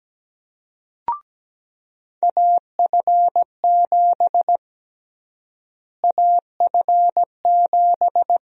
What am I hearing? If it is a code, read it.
Morse code, heard as a keyed tone, about 17 wpm, 700 Hz. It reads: AF7 AF7
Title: USA Call Sign Prefixes - Repeated 2x - 17wpm